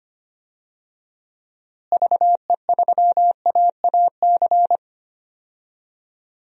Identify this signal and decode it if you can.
Morse code, heard as a keyed tone, about 25 wpm, 700 Hz. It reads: VE3AAC
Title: DXCC-100 Call Signs - 6 Characters - Repeated 3x - 25wpm